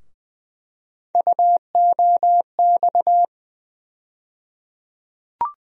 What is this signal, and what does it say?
Morse code, heard as a keyed tone, about 20 wpm, 700 Hz. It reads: UOX
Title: USA Call Sign Postfixes - 3 Letters 20wpm